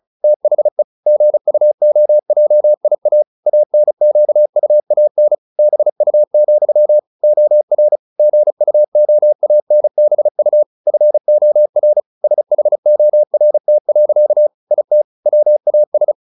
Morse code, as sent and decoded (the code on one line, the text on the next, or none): - .... / --. ..- --- .--- .. .- / .- -. --.- ..- .- -. / -... ..- --..-- / --- .-. / --. ..- --- .- -. -... ..- / ..-. --- .-. / ... .... --- .-. - .-.-.- / .. - / .-- .- ...
THE GUOJIA ANQUAN BU, OR GUOANBU FOR SHORT. IT WAS